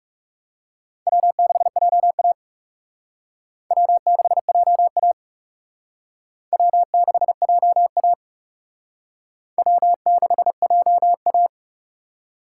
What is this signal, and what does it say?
Morse code, heard as a keyed tone, about 37 wpm, 700 Hz. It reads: W6JA W6JA W6JA W6JA
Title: Speed Racing - USA 1x2 Call Signs 30wpm